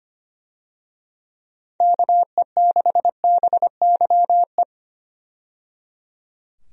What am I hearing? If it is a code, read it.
Morse code, heard as a keyed tone, about 25 wpm, 700 Hz. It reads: KE6BYE
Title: USA Call Signs - 6 Characters - 25wpm